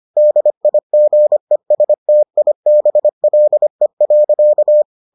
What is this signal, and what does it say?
Morse code, heard as a keyed tone, about 25 wpm, 600 Hz. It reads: DIGESTIBLE.